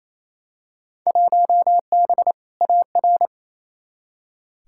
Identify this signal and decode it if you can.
Morse code, heard as a keyed tone, about 28 wpm, 700 Hz. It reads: 1B AR
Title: Field Day Exchanges Encore - 28wpm